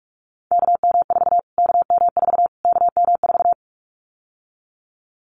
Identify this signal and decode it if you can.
Morse code, heard as a keyed tone, about 44 wpm, 700 Hz. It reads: KM4 KM4 KM4
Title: Callsign Prefixes for Top 25 DXCC - Repeated 3x - 45wpm